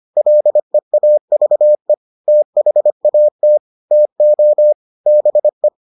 Morse code, heard as a keyed tone, about 25 wpm, 600 Hz. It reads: LEAVE THAT TO BE